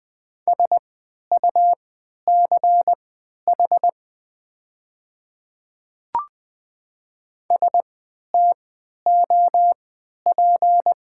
Morse code, heard as a keyed tone, about 20 wpm, spaced as for 11 wpm, 700 Hz. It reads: SUCH STOP
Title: 3x Character Spacing - Top 300 Words - 20wpm